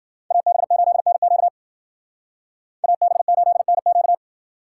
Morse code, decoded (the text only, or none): AB8NQ AB8NQ